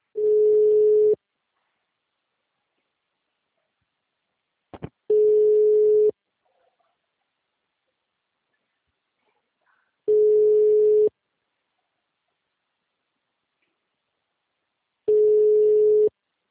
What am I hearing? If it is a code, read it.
Ringback tone (Europe, 425 Hz).
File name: out- -2006- - - .70113